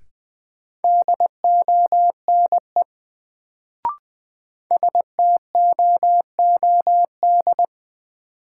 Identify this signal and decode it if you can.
Morse code, heard as a keyed tone, about 21 wpm, 700 Hz. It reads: DONE STOOD
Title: Sentences from Top 500 Words - New Word Drill - 20wpm